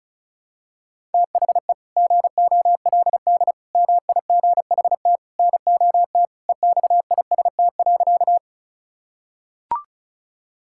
Morse code, heard as a keyed tone, about 34 wpm, 690 Hz. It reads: THE GOLD MIGHT NOT EXIST.
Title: Sentences from Top 1000 Words - Review Part 5 - 35wpm